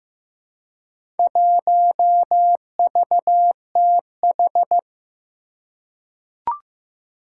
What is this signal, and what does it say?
Morse code, heard as a keyed tone, about 15 wpm, 700 Hz. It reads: 1VTH
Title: Four Challenging Letter-Number Combinations 15wpm